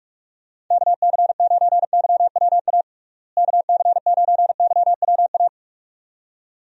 Morse code, heard as a keyed tone, about 46 wpm, 700 Hz. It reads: KC9YWA KC9YWA
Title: USA Call Signs - 6 Characters - Repeated 2x - 45wpm